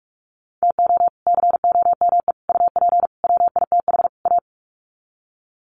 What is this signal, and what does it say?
Morse code, heard as a keyed tone, about 44 wpm, 700 Hz. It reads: TO COME UP WITH A
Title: Sets of 5 Words 45wpm